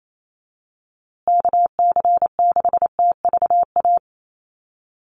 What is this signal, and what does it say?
Morse code, heard as a keyed tone, about 28 wpm, 700 Hz. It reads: KC6TVA
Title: USA Call Signs Repeated 3x 28wpm